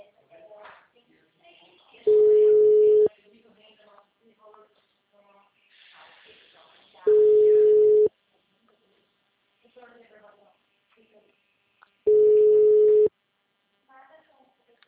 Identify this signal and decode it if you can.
Ringback tone (Europe, 425 Hz), with speech.
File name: out- -2009- - - .11462